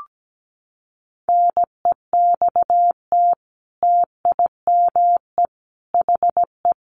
Morse code, heard as a keyed tone, about 17 wpm, 700 Hz. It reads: NEXT TIME HE
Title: A Sound Of Thunder 17wpm